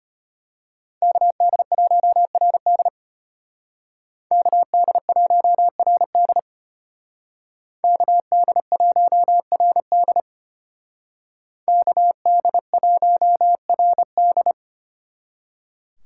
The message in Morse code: KD1RD KD1RD KD1RD KD1RD

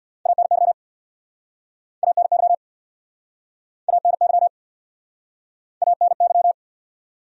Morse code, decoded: ANY ANY ANY ANY

